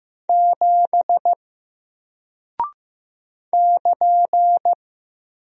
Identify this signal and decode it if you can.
Morse code, heard as a keyed tone, about 15 wpm, 700 Hz. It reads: 7 <KN>
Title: Single Character (Letter:Number:Punctuation:Prosign) Alt 15wpm